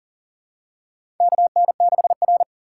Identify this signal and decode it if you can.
Morse code, heard as a keyed tone, about 40 wpm, 700 Hz. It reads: KN6R
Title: USA Call Signs - 4 Characters - 3x Character Spacing - 40wpm